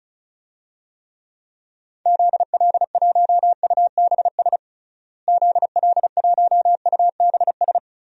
Morse code, heard as a keyed tone, about 36 wpm, 700 Hz. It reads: ZL1UBS ZL1UBS
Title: DXCC-100 Call Signs - 6 Characters - Repeated 3x - 35wpm